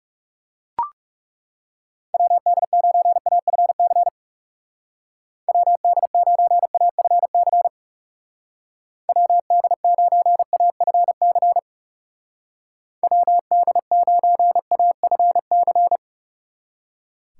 Morse code, heard as a keyed tone, about 38 wpm, 700 Hz. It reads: WD9AFC WD9AFC WD9AFC WD9AFC